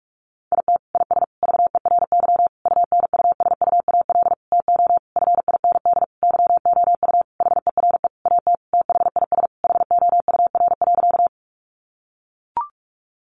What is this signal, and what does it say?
Morse code, heard as a keyed tone, about 44 wpm, 700 Hz. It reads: IT IS VERY UNUSUAL TO FIND YOU HERE AT THIS HOUR.
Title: Sentences from Top 1400 Words 45wpm